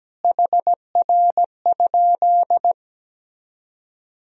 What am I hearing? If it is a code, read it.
Morse code, heard as a keyed tone, about 17 wpm, 700 Hz. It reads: HR?
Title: Sentences from Top 200 Words and Abbreviations 17wpm